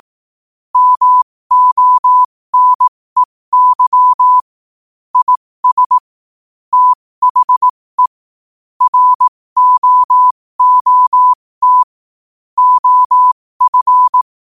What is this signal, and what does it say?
Morse code, heard as a keyed tone, about 18 wpm, spaced as for 15 wpm, 1000 Hz. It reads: MONEY IS THE ROOT OF
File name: qotdfiles / 15wpm